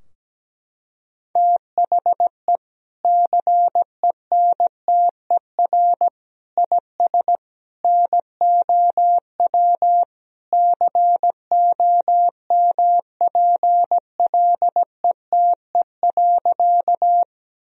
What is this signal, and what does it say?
Morse code, heard as a keyed tone, about 17 wpm, 700 Hz. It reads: THE CENTER IS NOW COMPLETE.
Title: Sentences from Top 1000 Words - Review Part 2 - 17wpm